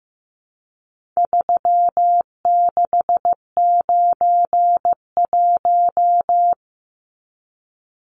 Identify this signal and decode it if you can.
Morse code, heard as a keyed tone, about 15 wpm, 700 Hz. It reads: 3691
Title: Numbers - 4-Digit 15wpm